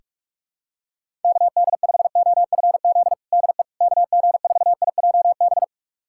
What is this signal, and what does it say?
Morse code, heard as a keyed tone, about 46 wpm, 700 Hz. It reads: KD5OFZ DE KG4IJB
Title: USA Callsign DE Callsign - 6 Characters - 45wpm